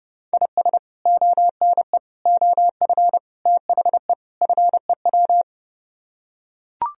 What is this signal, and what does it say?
Morse code, heard as a keyed tone, about 29 wpm, 700 Hz. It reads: IS ONE OF THE FEW